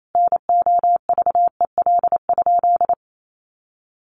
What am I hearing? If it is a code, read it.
Morse code, heard as a keyed tone, about 28 wpm, 700 Hz. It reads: NOVEL?
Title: Sentences from Top 1300 Words 28wpm